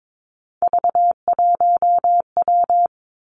Morse code, heard as a keyed tone, about 22 wpm, 700 Hz. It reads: V1W
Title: Three Challenging Letter-Number Combinations 22wpm